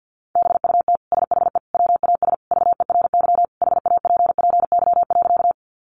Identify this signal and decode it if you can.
Morse code, heard as a keyed tone, about 48 wpm, 700 Hz. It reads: BUT SHE WAS VERY HAPPY.
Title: Sentences from Top 1000 Words 50wpm